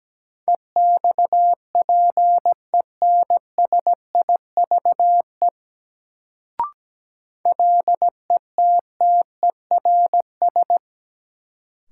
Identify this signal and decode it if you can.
Morse code, heard as a keyed tone, about 17 wpm, 700 Hz. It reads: EXPENSIVE LETTERS
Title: Sentences from Top 1000 Words - Review Part 6 - New Word Drill - 17wpm